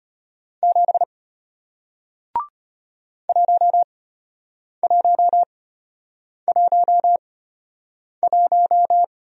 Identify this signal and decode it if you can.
Morse code, heard as a keyed tone, about 34 wpm, spaced as for 8 wpm, 700 Hz. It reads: 7 1111